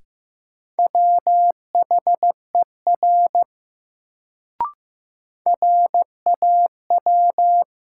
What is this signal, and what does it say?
Morse code, heard as a keyed tone, about 15 wpm, 700 Hz. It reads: WHER RAW
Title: Sentences from Top 1500 Words - New Word Drill - 15wpm